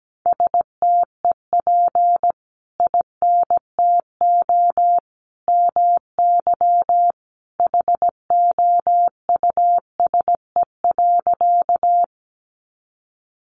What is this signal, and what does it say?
Morse code, heard as a keyed tone, about 17 wpm, 700 Hz. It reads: STEP INTO MY HOUSE.
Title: Sentences from Top 600 Words 17wpm